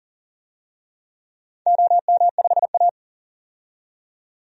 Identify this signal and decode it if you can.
Morse code, heard as a keyed tone, about 40 wpm, 700 Hz. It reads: OM5A
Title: DXCC-100 Call Signs - 4 Characters - 40wpm